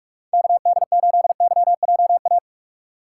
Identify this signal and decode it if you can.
Morse code, heard as a keyed tone, about 43 wpm, 700 Hz. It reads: KD8YJA
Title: USA Call Signs - 6 Characters - 3x Character Spacing - 45wpm